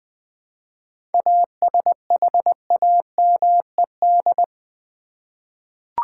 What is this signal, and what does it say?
Morse code, heard as a keyed tone, about 20 wpm, 700 Hz. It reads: ASHAMED
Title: 7 Letter Words - First 3 Letters Given - 20wpm